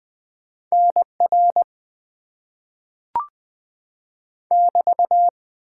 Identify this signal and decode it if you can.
Morse code, heard as a keyed tone, about 20 wpm, 700 Hz. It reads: NR <BT>
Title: Top 90 QSO Elements - 6x Character Spacing - 20wpm